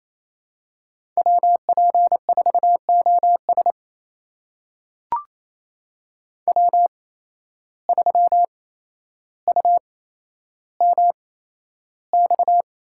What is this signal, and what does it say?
Morse code, heard as a keyed tone, about 27 wpm, spaced as for 12 wpm, 700 Hz. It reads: WP4OS W3UMX
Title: USA Call Signs - 5 Characters - 8x Character Spacing - 28wpm